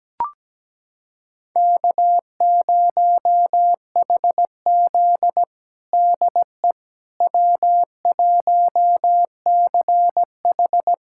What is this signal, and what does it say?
Morse code, heard as a keyed tone, about 17 wpm, 700 Hz. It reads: K0HZ DE W1CH
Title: USA Callsign DE Callsign - 4 Characters - 17wpm